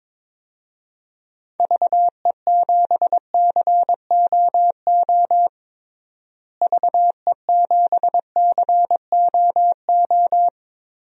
Morse code, heard as a keyed tone, about 22 wpm, 700 Hz. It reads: VE7COO VE7COO
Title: DXCC-100 Call Signs - 6 Characters - Repeated 3x - 22wpm